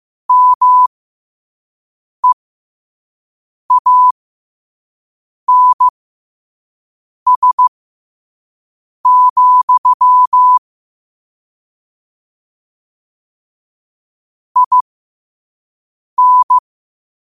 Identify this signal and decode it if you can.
Morse code, heard as a keyed tone, about 15 wpm, spaced as for 5 wpm, 1000 Hz. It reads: MEANS, IN
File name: qotdfiles / 5wpm